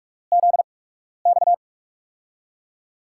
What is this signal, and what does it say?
Morse code, heard as a keyed tone, about 44 wpm, spaced as for 12 wpm, 700 Hz. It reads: ZX
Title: USA Call Signs - 8x Character Spacing - 45wpm